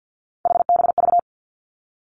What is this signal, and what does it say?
Morse code, heard as a keyed tone, about 50 wpm, 700 Hz. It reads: HBV
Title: Three Challenging Letter-Number Combinations 50wpm